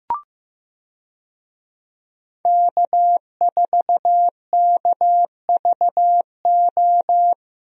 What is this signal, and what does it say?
Morse code, heard as a keyed tone, about 15 wpm, 700 Hz. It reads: K4KVO
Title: USA Call Signs - 5 Characters - Repeated 3x - 15wpm